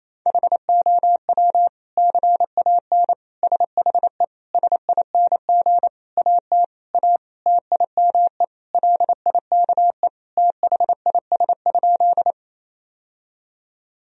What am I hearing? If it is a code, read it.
Morse code, heard as a keyed tone, about 28 wpm, 700 Hz. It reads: HOW CAN SHE SING AT A TIME LIKE THIS?
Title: Sentences from Top 1000 Words - Review Part 2 - 28wpm